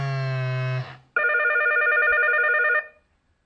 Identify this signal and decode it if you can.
There is no code.